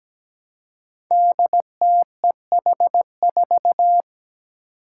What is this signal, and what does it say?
Morse code, heard as a keyed tone, about 17 wpm, 700 Hz. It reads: DTEH4
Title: Five Challenging Letter-Number Combinations 17wpm